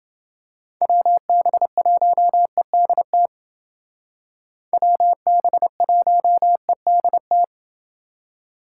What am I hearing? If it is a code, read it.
Morse code, heard as a keyed tone, about 28 wpm, 700 Hz. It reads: WB1EDT WB1EDT